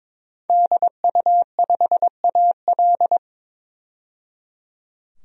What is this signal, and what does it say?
Morse code, heard as a keyed tone, about 22 wpm, 700 Hz. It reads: DU5AL
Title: DXCC-100 Call Signs - 5 Characters - 22wpm